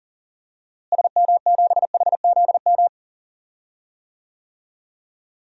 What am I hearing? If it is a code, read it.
Morse code, heard as a keyed tone, about 40 wpm, 700 Hz. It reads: SM7HZM
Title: DXCC-100 Call Signs - 6 Characters - Repeated 3x - 40wpm